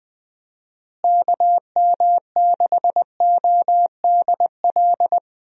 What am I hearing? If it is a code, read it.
Morse code, heard as a keyed tone, about 20 wpm, 700 Hz. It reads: KM6ODL
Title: USA Call Signs - 6 Characters - Repeated 2x - 20wpm